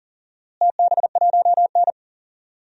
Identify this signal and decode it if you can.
Morse code, heard as a keyed tone, about 40 wpm, 700 Hz. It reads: TB1N
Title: DXCC-100 Call Signs - 4 Characters - Repeated 2x - 40wpm